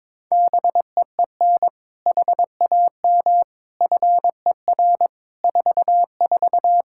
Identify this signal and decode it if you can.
Morse code, heard as a keyed tone, about 22 wpm, 700 Hz. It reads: BEEN HAM FER 44
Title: Practice QSO - Exchange 3 - More Info - 22wpm